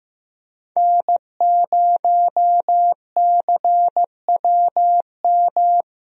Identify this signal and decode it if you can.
Morse code, heard as a keyed tone, about 15 wpm, 700 Hz. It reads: N0CWM
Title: USA Call Signs - 5 Characters - Repeated 3x - 15wpm